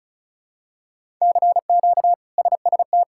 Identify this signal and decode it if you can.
Morse code, heard as a keyed tone, about 36 wpm, 700 Hz. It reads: CQ SST
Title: K1USN SST QSO Practice - 35wpm